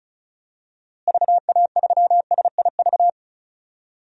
Morse code, heard as a keyed tone, about 34 wpm, 700 Hz. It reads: VA3SIV
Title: DXCC-100 Call Signs - 6 Characters - Repeated 2x - 35wpm